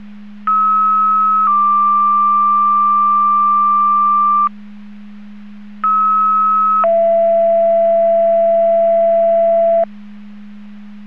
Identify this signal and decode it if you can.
Two-tone page, heard as a one-second tone, then a three-second tone, about 1290 then 1180 Hz, about 1290 then 670 Hz. It is heard from 0.5 to 9.8 s.
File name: Portage / 03039 19-57 Portage 154.385 Paging